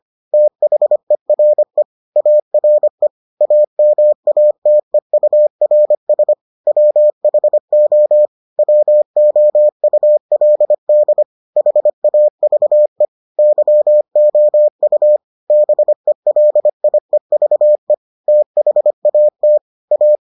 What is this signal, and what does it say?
Morse code, heard as a keyed tone, about 25 wpm, 600 Hz. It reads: THERE ARE AMATEURS WHO WOULD HAVE YOU BELIEVE THAT A